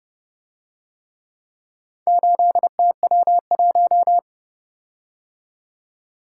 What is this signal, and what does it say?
Morse code, heard as a keyed tone, about 29 wpm, 700 Hz. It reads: 8TW1